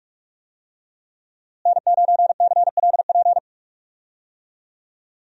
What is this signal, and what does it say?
Morse code, heard as a keyed tone, about 46 wpm, 700 Hz. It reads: N9CLP
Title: USA Call Signs - 5 Characters - Repeated 2x - 45wpm